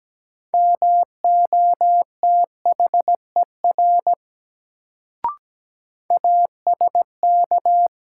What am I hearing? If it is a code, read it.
Morse code, heard as a keyed tone, about 17 wpm, 700 Hz. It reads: MOTHER ASK
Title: Sentences from Top 200 Words - New Word Drill - 17wpm